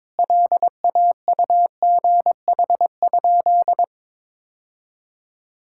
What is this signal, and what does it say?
Morse code, heard as a keyed tone, about 21 wpm, 700 Hz. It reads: LAUGH?
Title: Sentences from Top 1000 Words - Review Part 5 - 22wpm